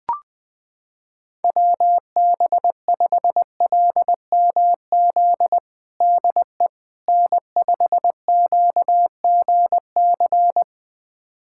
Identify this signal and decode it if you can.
Morse code, heard as a keyed tone, about 21 wpm, 700 Hz. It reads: WB5LMZ DE N5QGC
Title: USA Callsign DE Callsign 20wpm